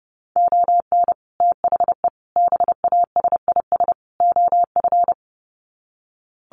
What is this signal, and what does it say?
Morse code, heard as a keyed tone, about 30 wpm, 700 Hz. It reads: ON THE BASIS OF
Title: Sets of 4 Words 30wpm